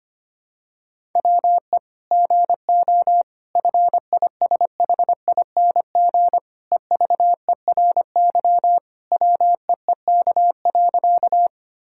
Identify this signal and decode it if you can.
Morse code, heard as a keyed tone, about 25 wpm, 700 Hz. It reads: WE GO FISHING EVERY WEEK.
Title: Sentences from Top 1000 Words - Review Part 5 - 25wpm